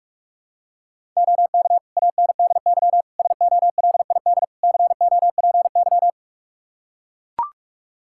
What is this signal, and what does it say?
Morse code, heard as a keyed tone, about 45 wpm, 700 Hz. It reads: OK ANDY SOLID COPY
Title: K1USN SST QSO Practice 45wpm